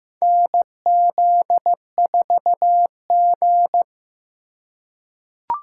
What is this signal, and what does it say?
Morse code, heard as a keyed tone, about 15 wpm, 700 Hz. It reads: NZ4G
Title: USA Call Signs - 4 Characters - 2x Character Spacing - 15wpm